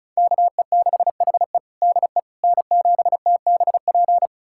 Morse code, heard as a keyed tone, about 34 wpm, 700 Hz. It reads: KE6HE DE N7TBP
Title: USA Callsign DE Callsign - 5 Characters - 35wpm